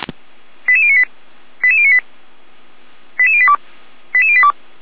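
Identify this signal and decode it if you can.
Five-tone selective call (ZVEI), heard as quick runs of five tones, tones of about 70 ms. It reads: 88098 88098 88082 88082
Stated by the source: ZVEI five-tone selective call received by SDR.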